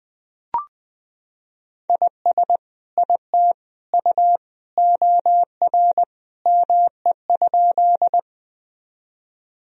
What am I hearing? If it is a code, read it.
Morse code, heard as a keyed tone, about 20 wpm, 700 Hz. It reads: IS IT U OR ME?
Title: Sentences from Top 200 Words and Abbreviations 20wpm